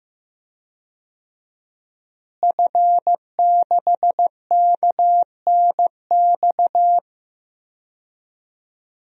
Morse code, heard as a keyed tone, about 15 wpm, 700 Hz. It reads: F6KNX
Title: DXCC-100 Call Signs - 5 Characters - Repeated 3x - 15wpm